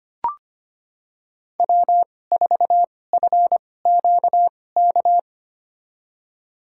Morse code, heard as a keyed tone, about 25 wpm, spaced as for 18 wpm, 700 Hz. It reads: W4FQK
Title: USA Call Signs - 5 Characters - 2x Character Spacing - 25wpm